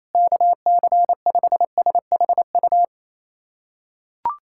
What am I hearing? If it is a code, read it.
Morse code, heard as a keyed tone, about 28 wpm, 700 Hz. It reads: KC5SHU
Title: USA Callsign DE Callsign - 6 Characters - 28wpm